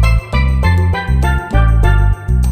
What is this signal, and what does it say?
There is no code.